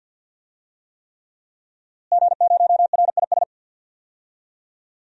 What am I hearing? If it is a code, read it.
Morse code, heard as a keyed tone, about 50 wpm, 700 Hz. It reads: G0RIS
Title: DXCC-100 Call Signs - 5 Characters - Repeated 3x - 50wpm